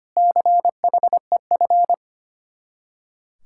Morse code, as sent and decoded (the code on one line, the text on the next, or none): -.-. .... ..-.
CHEF